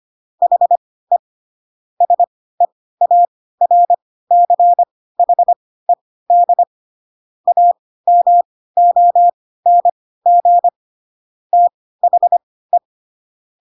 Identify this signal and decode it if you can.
Morse code, heard as a keyed tone, about 25 wpm, spaced as for 16 wpm, 700 Hz. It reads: HE SEARCHED AMONG THE
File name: Pinocchio / Pinocchio ch1 10 25 ik1hge